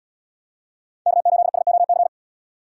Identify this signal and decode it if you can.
Morse code, heard as a keyed tone, about 77 wpm, 700 Hz. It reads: K8SOC